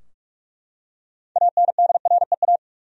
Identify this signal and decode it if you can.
Morse code, heard as a keyed tone, about 41 wpm, 700 Hz. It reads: ANDREA
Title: USA Common Names - 45wpm